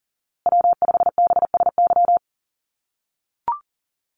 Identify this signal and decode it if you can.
Morse code, heard as a keyed tone, about 38 wpm, 700 Hz. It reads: W5BSY
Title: USA Call Signs Repeated 2x 40wpm